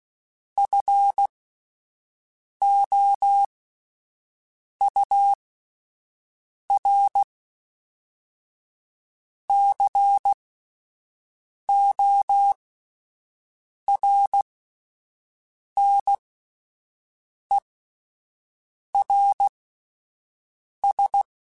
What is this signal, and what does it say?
Morse code, heard as a keyed tone, about 16 wpm, spaced as for 5 wpm, 800 Hz. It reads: FOURCORNERS